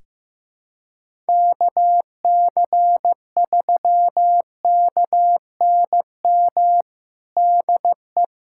Morse code, heard as a keyed tone, about 15 wpm, 700 Hz. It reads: KC3KNM DE
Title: USA Callsign DE Callsign - 6 Characters - 15wpm